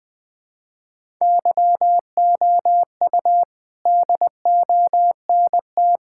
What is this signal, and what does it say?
Morse code, heard as a keyed tone, about 20 wpm, 700 Hz. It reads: YOU DONT
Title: Sets of 8 Words - 20wpm